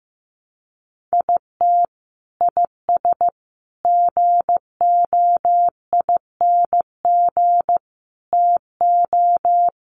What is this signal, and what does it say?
Morse code, heard as a keyed tone, about 15 wpm, 700 Hz. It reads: IT IS GOING TO
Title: Sets of 5 Words 15wpm